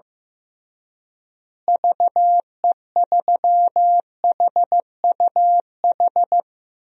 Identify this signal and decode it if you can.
Morse code, heard as a keyed tone, about 15 wpm, 700 Hz. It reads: VE3HUH
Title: DXCC-100 Call Signs - 6 Characters - Repeated 3x - 15wpm